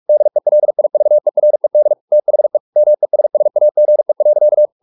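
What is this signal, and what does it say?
Morse code, heard as a keyed tone, about 44 wpm, 600 Hz. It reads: DELIVERED THE MESSAGE.
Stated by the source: Morse practice at 45 wpm, the novel Never by Ken Follett in English, from CW med Gnister.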